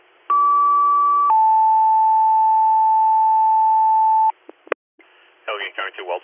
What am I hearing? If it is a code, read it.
Two-tone page, heard as a one-second tone, then a three-second tone, about 1150 then 880 Hz, with speech. It is heard from 0.3 to 4.3 s.